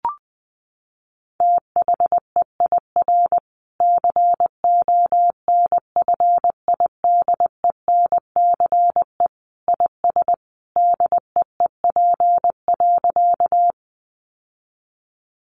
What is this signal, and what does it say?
Morse code, heard as a keyed tone, about 20 wpm, 700 Hz. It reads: THEIR CONFIDENCE IS DEEP.
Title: Sentences from Top 1500 Words 20wpm